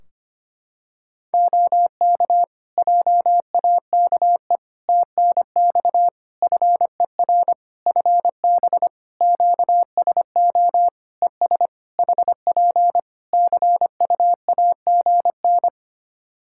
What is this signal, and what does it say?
Morse code, heard as a keyed tone, about 25 wpm, 700 Hz. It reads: OK JAKE TNX FER FB QSO ES HP CUAGN
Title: Practice QSO - Exchange 4 - Ending - 25wpm